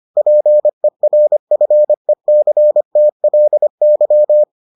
Morse code, heard as a keyed tone, about 25 wpm, 600 Hz. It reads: PERFECTLY